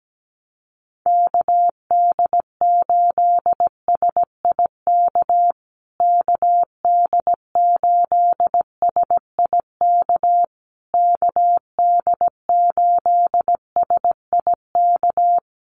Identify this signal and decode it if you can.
Morse code, heard as a keyed tone, about 17 wpm, 700 Hz. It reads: KD8SIK KD8SIK KD8SIK
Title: USA Call Signs Repeated 3x 17wpm